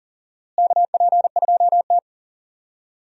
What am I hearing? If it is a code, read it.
Morse code, heard as a keyed tone, about 40 wpm, 700 Hz. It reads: KP2T
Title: DXCC-100 Call Signs - 4 Characters - Repeated 3x - 40wpm